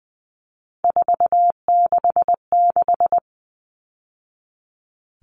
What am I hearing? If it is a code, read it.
Morse code, heard as a keyed tone, about 20 wpm, 700 Hz. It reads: 466
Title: Numbers - 3-Digit 20wpm